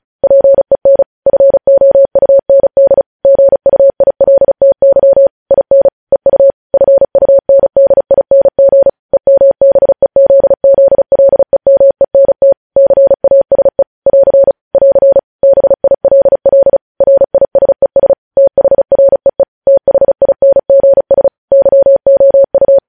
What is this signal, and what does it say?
Morse code, heard as a keyed tone, about 36 wpm, 550 Hz. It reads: PEN FOUND GUILTY IN EU FUNDING EMBEZZLEMENT CASE <AR> <AR> BILL RISES THREE THINGS YOU
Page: News headlines in Morse code at 35 wpm